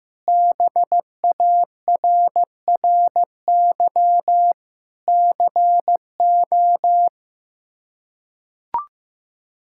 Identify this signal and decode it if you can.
Morse code, heard as a keyed tone, about 15 wpm, 700 Hz. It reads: BARRY CO
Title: K1USN SST QSO Practice - 15wpm